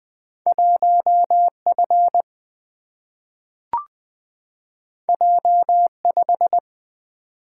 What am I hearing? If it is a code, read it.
Morse code, heard as a keyed tone, about 20 wpm, 700 Hz. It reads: 1 F J 5
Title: Single-Letter Single-Number 20wpm